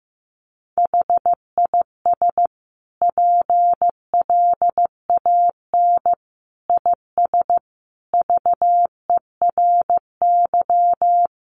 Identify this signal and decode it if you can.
Morse code, heard as a keyed tone, about 14 wpm, 700 Hz. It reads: HIS PLAN IS VERY
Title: Sentences from Top 500 Words 15wpm